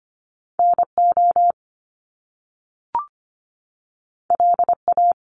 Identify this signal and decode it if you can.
Morse code, heard as a keyed tone, about 25 wpm, 700 Hz. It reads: NO LA
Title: Top Two Letter Combinations in Words 25wpm